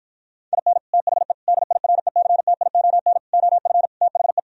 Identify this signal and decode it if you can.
Morse code, heard as a keyed tone, about 52 wpm, 700 Hz. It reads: IN THE DIRECTION OF THE